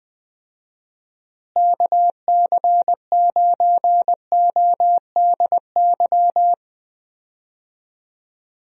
Morse code, heard as a keyed tone, about 20 wpm, 700 Hz. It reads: KC9ODY
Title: USA Call Signs - 6 Characters - Repeated 3x - 20wpm